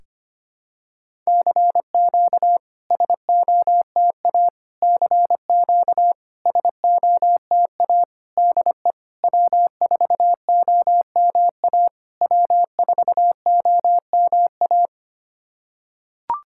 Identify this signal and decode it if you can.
Morse code, heard as a keyed tone, about 25 wpm, 690 Hz. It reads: CQ SOTA CQ SOTA DE W4OMA W4OMA
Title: SOTA QSO Practice 25wpm